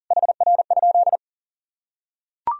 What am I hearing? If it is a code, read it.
Morse code, heard as a keyed tone, about 40 wpm, 700 Hz. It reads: HR?
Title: Sentences from Top 400 Words and Abbreviations 40wpm